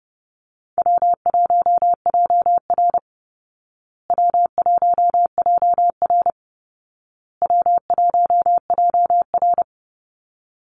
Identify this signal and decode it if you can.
Morse code, heard as a keyed tone, about 30 wpm, 700 Hz. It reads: W1JR W1JR W1JR
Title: Call Signs Repeated 2x - 30wpm